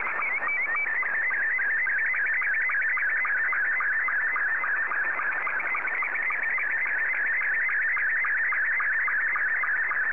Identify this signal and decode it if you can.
SSTV (slow-scan television), Scottie 2.